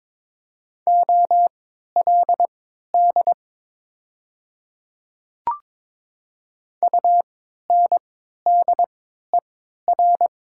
Morse code, heard as a keyed tone, about 22 wpm, spaced as for 12 wpm, 700 Hz. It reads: OLD UNDER
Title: 3x Character Spacing - Top 300 Words - 22wpm